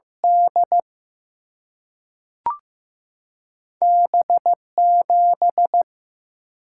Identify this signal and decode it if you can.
Morse code, heard as a keyed tone, about 15 wpm, 700 Hz. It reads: D B 7
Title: Single-Letter Single-Number 15wpm